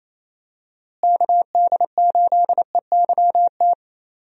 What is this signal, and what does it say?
Morse code, heard as a keyed tone, about 28 wpm, 700 Hz. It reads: KD8EYT
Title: USA Call Signs - 6 Characters - Repeated 3x - 28wpm